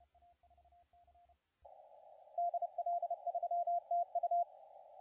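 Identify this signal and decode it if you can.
Morse code, heard as a keyed tone, about 27 wpm, 690 Hz, with radio noise. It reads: DL3TU
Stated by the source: CW contest audio recorded at LA4C during the SAC CW contest.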